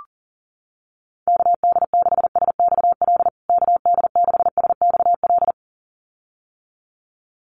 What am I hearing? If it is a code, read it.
Morse code, heard as a keyed tone, about 40 wpm, 700 Hz. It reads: KD6SXL KD6SXL
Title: USA Call Signs Repeated 2x 40wpm